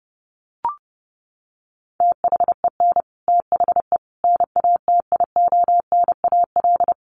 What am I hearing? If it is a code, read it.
Morse code, heard as a keyed tone, about 30 wpm, 700 Hz. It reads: THEN THE NATIONAL
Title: Sentences from Top 1500 Words 30wpm